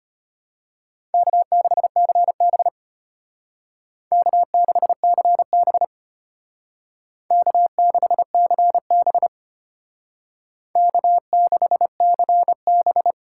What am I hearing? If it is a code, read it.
Morse code, heard as a keyed tone, about 32 wpm, 700 Hz. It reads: K6CB K6CB K6CB K6CB